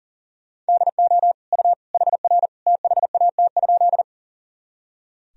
Morse code, heard as a keyed tone, about 40 wpm, 700 Hz. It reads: DO U HR THAT?
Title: Sentences from Top 300 Words and Abbreviations 40wpm